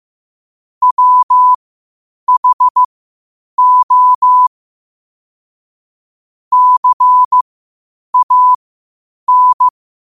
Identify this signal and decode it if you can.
Morse code, heard as a keyed tone, about 15 wpm, spaced as for 8 wpm, 1000 Hz. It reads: WHO CAN
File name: qotdfiles / 8wpm